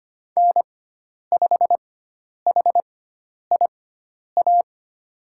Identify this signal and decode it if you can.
Morse code, heard as a keyed tone, about 25 wpm, spaced as for 10 wpm, 700 Hz. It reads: N5HIA
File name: Lesson 16g - Letter c - Call Signs 25wpm